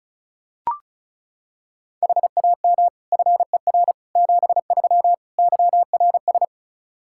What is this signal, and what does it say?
Morse code, heard as a keyed tone, about 36 wpm, 700 Hz. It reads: HAM FER 73 YRS
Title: Practice QSO - Exchange 3 - More Info - 35wpm